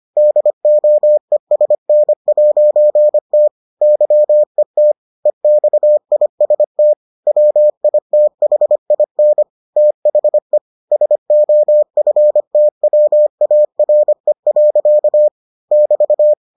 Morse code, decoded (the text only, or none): DOESN'T YET EXIST WITHIN THE SOFTWARE. <BT>